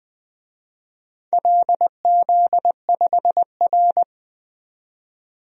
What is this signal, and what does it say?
Morse code, heard as a keyed tone, about 20 wpm, 700 Hz. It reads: LZ5R
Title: DXCC-100 Call Signs - 4 Characters - 20wpm